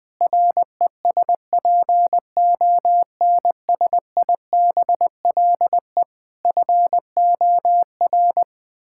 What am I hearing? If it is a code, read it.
Morse code, heard as a keyed tone, about 20 wpm, 700 Hz. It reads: RESPONSIBLE FOR
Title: Sentences from Top 1600 Words 20wpm